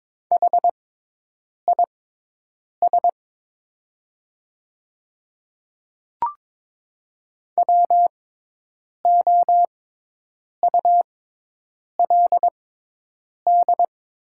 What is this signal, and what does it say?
Morse code, heard as a keyed tone, about 22 wpm, spaced as for 8 wpm, 700 Hz. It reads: HIS WOULD
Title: 6x Character Spacing - Top 100 Words - 22wpm